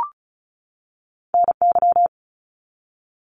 Morse code, heard as a keyed tone, about 36 wpm, 700 Hz. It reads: NY
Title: US States 35wpm